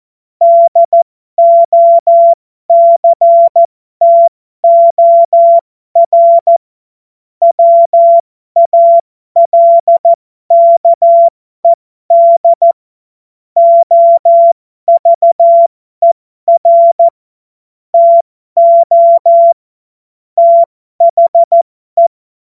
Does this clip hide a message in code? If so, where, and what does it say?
Yes: Morse code from 0.4 to 22.1 s, heard as a keyed tone, about 14 wpm, spaced as for 12 wpm, 680 Hz. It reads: DOCTOR WALKED OVER TO THE